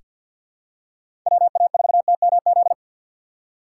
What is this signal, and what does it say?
Morse code, heard as a keyed tone, about 47 wpm, 700 Hz. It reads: WA4TMZ
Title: USA Call Signs - 3x Character Spacing - 50wpm